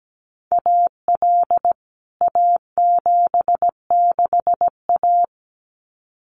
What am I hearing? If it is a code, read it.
Morse code, heard as a keyed tone, about 17 wpm, 700 Hz. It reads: AL A76A
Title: CWT Exchange 17wpm